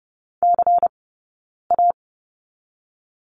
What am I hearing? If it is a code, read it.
Morse code, heard as a keyed tone, about 30 wpm, 700 Hz. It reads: C A
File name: 092 Lesson 16d - Letter c - Two Characters 30wpm